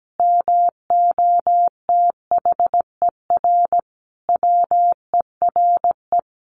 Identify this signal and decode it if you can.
Morse code, heard as a keyed tone, about 17 wpm, 700 Hz. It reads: MOTHER WERE
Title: Sentences from Top 1500 Words 17wpm